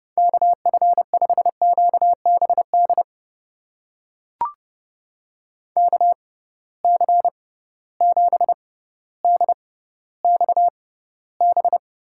Morse code, heard as a keyed tone, about 30 wpm, spaced as for 15 wpm, 700 Hz. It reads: KF5QBD KC7DXB
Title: USA Call Signs - 6 Characters - 6x Character Spacing - 30wpm